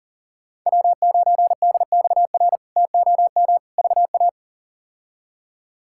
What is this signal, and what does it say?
Morse code, heard as a keyed tone, about 40 wpm, 700 Hz. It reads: W9DXR TOM VA
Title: K1USN SST QSO Practice - 40wpm